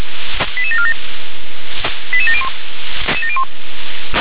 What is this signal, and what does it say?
Five-tone selective call (ZVEI), heard as quick runs of five tones, tones of about 70 ms. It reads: 88747 88701